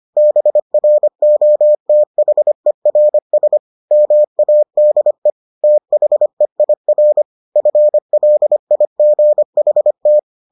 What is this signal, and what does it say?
Morse code, heard as a keyed tone, about 25 wpm, 600 Hz. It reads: BROTHERS MADE THEIR FLIGHT